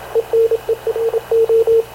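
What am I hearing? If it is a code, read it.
Morse code, heard as a keyed tone, about 27 wpm, 450 Hz, with radio noise. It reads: ENERO
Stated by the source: Havana Radio CLA press (PX) broadcast, recorded by Radio Officer Paul Dumesnil aboard mv CNAV QUEST/CZDO.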